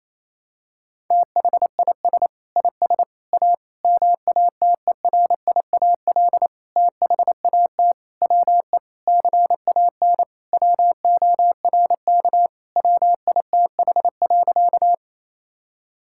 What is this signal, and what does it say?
Morse code, heard as a keyed tone, about 28 wpm, 700 Hz. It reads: THIS IS A MATERIAL THAT WE CAN WORK WITH.
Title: Sentences from Top 1000 Words - Review Part 5 - 28wpm